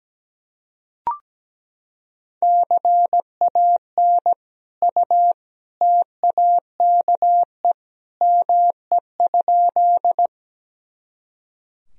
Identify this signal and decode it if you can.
Morse code, heard as a keyed tone, about 17 wpm, 700 Hz. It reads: CAN U TAKE ME?
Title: Sentences from Top 200 Words and Abbreviations 17wpm